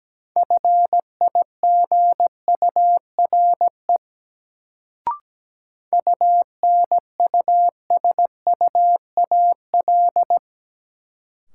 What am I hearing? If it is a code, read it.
Morse code, heard as a keyed tone, about 17 wpm, 700 Hz. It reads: FIGURE UNUSUAL
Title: Sentences from Top 1400 Words - New Word Drill Alt - 17wpm